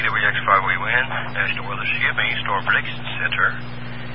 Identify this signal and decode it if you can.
Emergency Alert System (SAME) tones, heard as harsh buzzing bursts, with speech. It is heard from 0.6 to 3.6 s.